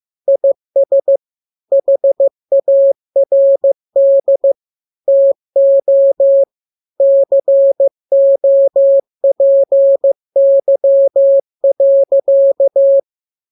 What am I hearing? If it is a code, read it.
Morse code, heard as a keyed tone, about 15 wpm, 550 Hz. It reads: IS HARD TO COPY.
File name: jokes / joke 4 15 0000